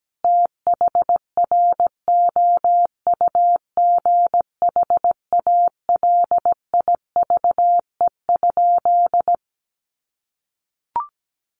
Morse code, heard as a keyed tone, about 17 wpm, 700 Hz. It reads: THROUGH ALIVE?
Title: A Sound Of Thunder 17wpm